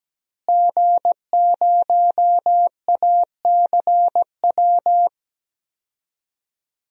Morse code, heard as a keyed tone, about 17 wpm, 700 Hz. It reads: G0ACW
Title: DXCC-100 Call Signs - 5 Characters - Repeated 3x - 17wpm